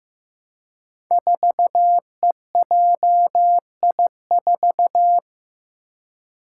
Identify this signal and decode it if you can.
Morse code, heard as a keyed tone, about 15 wpm, 700 Hz. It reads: 4EJI4
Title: Five Challenging Letter-Number Combinations 15wpm